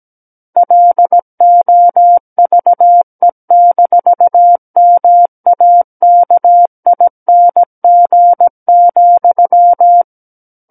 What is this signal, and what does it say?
Morse code, heard as a keyed tone, about 17 wpm, 700 Hz. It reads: LOVE-MAKING,
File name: Book One / 08-Chapter 8 - 17 WPM